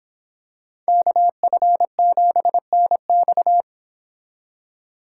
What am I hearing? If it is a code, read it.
Morse code, heard as a keyed tone, about 25 wpm, 700 Hz. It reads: KF7NX